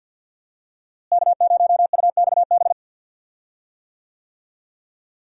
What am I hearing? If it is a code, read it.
Morse code, heard as a keyed tone, about 52 wpm, 700 Hz. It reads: K0UXB
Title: USA Call Signs - 5 Characters - Repeated 2x - 50wpm